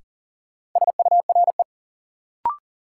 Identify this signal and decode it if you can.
Morse code, heard as a keyed tone, about 40 wpm, 700 Hz. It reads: SURE
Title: Sentences from Top 300 Words - New Word Drill - 40wpm